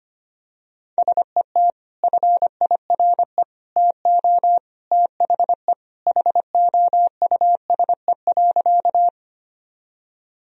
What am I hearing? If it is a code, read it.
Morse code, heard as a keyed tone, about 25 wpm, 700 Hz. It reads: SET FIRE TO THE HOUSE.
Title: Sentences from Top 1000 Words - Review Part 1 - 25wpm